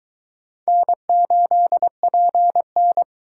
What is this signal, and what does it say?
Morse code, heard as a keyed tone, about 22 wpm, 700 Hz. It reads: N8PN